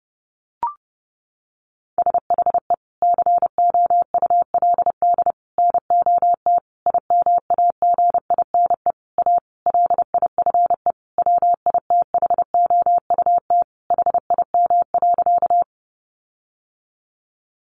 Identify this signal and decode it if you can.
Morse code, heard as a keyed tone, about 30 wpm, 700 Hz. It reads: SHE COULD NOT IMAGINE A LIFE WITHOUT HIM.
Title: Sentences from Top 1400 Words 30wpm